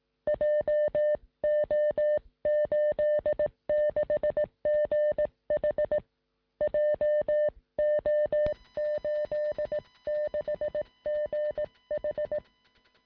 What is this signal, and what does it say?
Morse code, heard as a keyed tone, about 18 wpm, 600 Hz. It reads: JO86GH JO86GH
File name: sm7lcb / 1958z ok1ri 2m